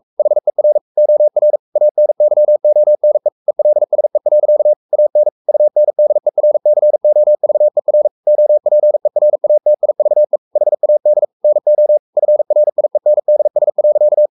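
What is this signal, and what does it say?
Morse code, heard as a keyed tone, about 44 wpm, 600 Hz. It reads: HER OR ANYONE ELSE. AN UNDERCOVER OPERATIVE HAD NO FRIENDS.
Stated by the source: Morse code at 43 wpm sending Ken Follett's novel Never in English, from CW med Gnister.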